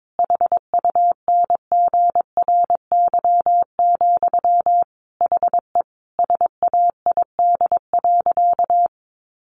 Morse code, heard as a keyed tone, about 22 wpm, 700 Hz. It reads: HUNGRY, HE SAID.
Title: Sentences from Top 1400 Words 22wpm